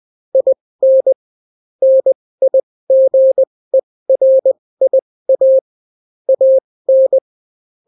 Morse code, heard as a keyed tone, about 20 wpm, spaced as for 16 wpm, 530 Hz. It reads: IN NIGERIA AN